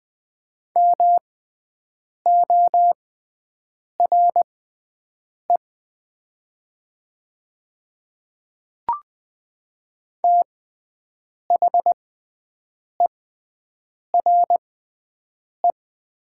Morse code, heard as a keyed tone, about 20 wpm, spaced as for 7 wpm, 700 Hz. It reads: MORE THERE
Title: 6x Character Spacing - Top 100 Words - 20wpm